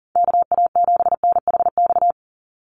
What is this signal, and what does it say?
Morse code, heard as a keyed tone, about 40 wpm, 700 Hz. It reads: KA7NHX
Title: USA Call Signs 40wpm